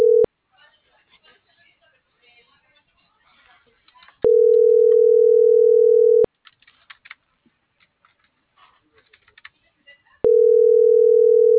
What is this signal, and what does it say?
Ringback tone (North America), 440 + 480 Hz.
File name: CENTRAL SAGOT / out danae.bsagot